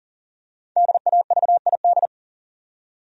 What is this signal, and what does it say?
Morse code, heard as a keyed tone, about 40 wpm, 700 Hz. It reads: DAVID
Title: USA Common Names - 40wpm